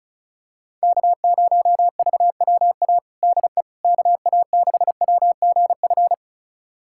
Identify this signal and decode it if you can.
Morse code, heard as a keyed tone, about 36 wpm, 700 Hz. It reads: K0VWA DE KA6WGF
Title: USA Callsign DE Callsign 35wpm